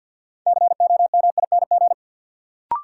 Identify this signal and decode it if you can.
Morse code, heard as a keyed tone, about 52 wpm, 700 Hz. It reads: COMING
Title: Sentences from Top 1300 Words - New Word Drill Alt - 50wpm